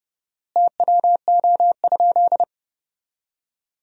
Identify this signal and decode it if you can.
Morse code, heard as a keyed tone, about 30 wpm, 700 Hz. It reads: TWO?
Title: 4x Word-Spacing - Sentences from Top 100 Words - 30wpm